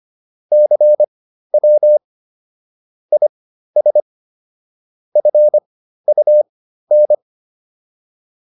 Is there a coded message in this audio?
Yes: Morse code from 0.5 to 7.1 s, heard as a keyed tone, about 25 wpm, spaced as for 13 wpm, 600 Hz. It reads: CW IS FUN